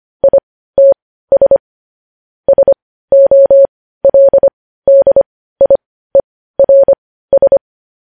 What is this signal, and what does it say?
Morse code, heard as a keyed tone, about 25 wpm, spaced as for 15 wpm, 550 Hz. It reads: ITS SOLDIERS
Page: news headlines in morse code at 15 wpm